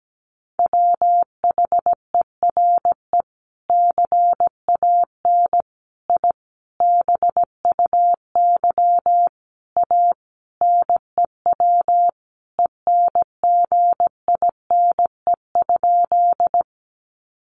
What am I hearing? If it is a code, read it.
Morse code, heard as a keyed tone, about 17 wpm, 700 Hz. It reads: WHERE CAN I BUY A NEW ENGINE?
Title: Sentences from Top 1500 Words 17wpm